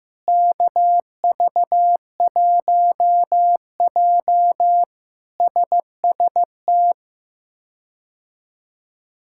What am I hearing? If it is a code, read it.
Morse code, heard as a keyed tone, about 16 wpm, 700 Hz. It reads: KV1J SST
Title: K1USN SST QSO Practice - 15wpm